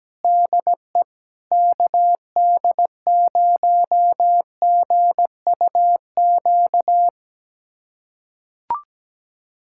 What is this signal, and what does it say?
Morse code, heard as a keyed tone, about 17 wpm, 700 Hz. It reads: DE KD0GUQ
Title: USA Callsign DE Callsign 17wpm